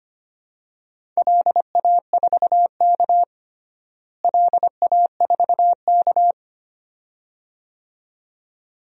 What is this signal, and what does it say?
Morse code, heard as a keyed tone, about 25 wpm, 700 Hz. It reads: LA4K LA4K
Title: DXCC-100 Call Signs - 4 Characters - Repeated 2x - 25wpm